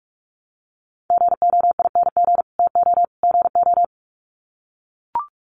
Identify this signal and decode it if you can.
Morse code, heard as a keyed tone, about 44 wpm, 700 Hz. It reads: GOING TO GO
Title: Sets of 3 Words 45wpm